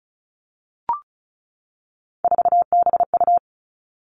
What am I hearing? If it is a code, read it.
Morse code, heard as a keyed tone, about 36 wpm, 700 Hz. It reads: 4BU
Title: Three Challenging Letter-Number Combinations 35wpm